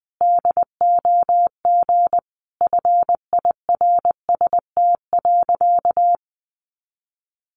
Morse code, heard as a keyed tone, about 20 wpm, 700 Hz. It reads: DOG FIRST.
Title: Sentences from Top 400 Words 20wpm